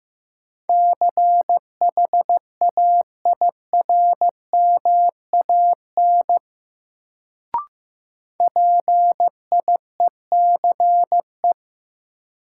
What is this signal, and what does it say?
Morse code, heard as a keyed tone, about 15 wpm, 700 Hz. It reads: CHAIRMAN PIECE
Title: Sets of 3 Words - New Word Drill - 15wpm